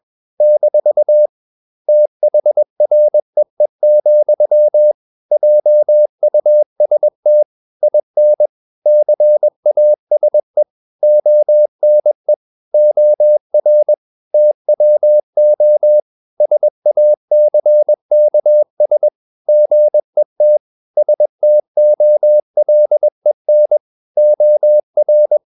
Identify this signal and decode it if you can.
Morse code, heard as a keyed tone, about 21 wpm, 600 Hz. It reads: - THREE, JUST IN CASE ONE OR TWO SACKS GET STOLEN OR
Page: CW med Gnister